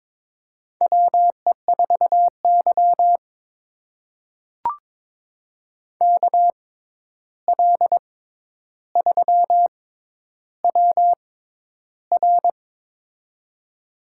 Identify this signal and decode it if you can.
Morse code, heard as a keyed tone, about 23 wpm, spaced as for 11 wpm, 700 Hz. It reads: WE4Y KL3WR
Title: USA Call Signs - 6x Character Spacing - 22wpm